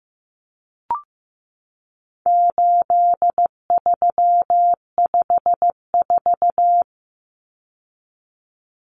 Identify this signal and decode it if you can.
Morse code, heard as a keyed tone, about 15 wpm, 700 Hz. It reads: 8354
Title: Numbers - 4-Digit 15wpm